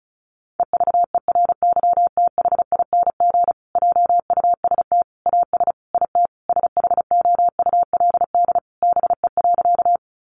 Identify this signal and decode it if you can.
Morse code, heard as a keyed tone, about 34 wpm, 700 Hz. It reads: EVERYTHING JUST AS IT SHOULD BE.
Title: Sentences from Top 700 Words 35wpm